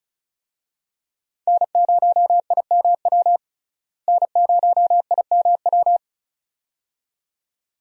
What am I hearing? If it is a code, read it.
Morse code, heard as a keyed tone, about 34 wpm, 700 Hz. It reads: N0IMW N0IMW
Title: USA Call Signs - 5 Characters - Repeated 2x - 35wpm